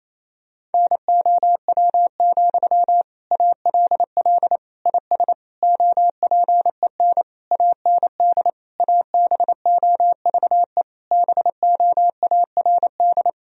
Morse code, heard as a keyed tone, about 28 wpm, 700 Hz. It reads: NOW, ALL IS OPEN AND ABOVE BOARD
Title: Sentences from Top 1000 Words - Review Part 4 - 28wpm